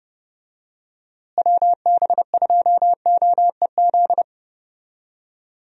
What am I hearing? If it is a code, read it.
Morse code, heard as a keyed tone, about 30 wpm, 700 Hz. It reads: WB2OEZ
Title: USA Call Signs - 6 Characters - Repeated 2x - 30wpm